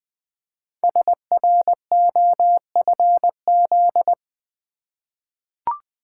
Morse code, heard as a keyed tone, about 20 wpm, 700 Hz. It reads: SROFZ
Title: Code Groups - 5 Letters - 20wpm